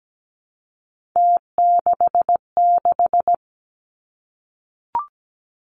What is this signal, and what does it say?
Morse code, heard as a keyed tone, about 17 wpm, 700 Hz. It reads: T66
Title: Three Challenging Letter-Number Combinations 17wpm